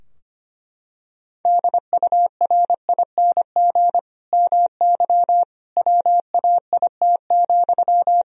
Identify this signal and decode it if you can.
Morse code, heard as a keyed tone, about 25 wpm, 700 Hz. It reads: DURING MY WAIT,
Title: Sentences from Top 1000 Words - Review Part 2 - 25wpm